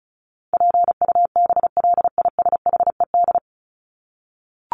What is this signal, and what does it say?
Morse code, heard as a keyed tone, about 34 wpm, 700 Hz. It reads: PUBLISHED
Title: ED Suffix - Top 100 Words with Base Words 35wpm